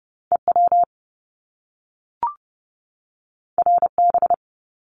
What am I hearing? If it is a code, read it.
Morse code, heard as a keyed tone, about 30 wpm, 700 Hz. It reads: EW RB
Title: Top Two Letter Combinations in Words 30wpm